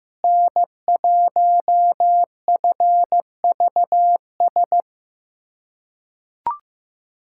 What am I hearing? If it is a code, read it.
Morse code, heard as a keyed tone, about 15 wpm, 700 Hz. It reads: N1FVS
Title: USA Callsign DE Callsign - 5 Characters - 15wpm